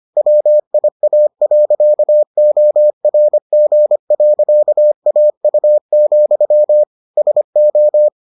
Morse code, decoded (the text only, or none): WIA.ORG.AU, SO